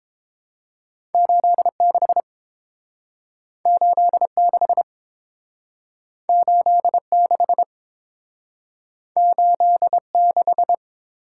Morse code, heard as a keyed tone, about 29 wpm, 700 Hz. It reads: 86 86 86 86